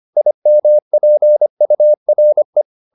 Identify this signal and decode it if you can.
Morse code, heard as a keyed tone, about 25 wpm, 600 Hz. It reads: IMPURE